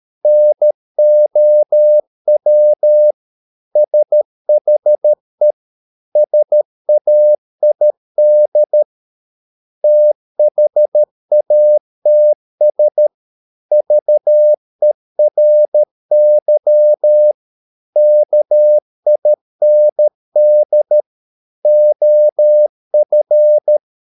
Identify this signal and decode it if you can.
Morse code, heard as a keyed tone, about 13 wpm, 600 Hz. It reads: NOW SHE SAID THATS VERY KIND OF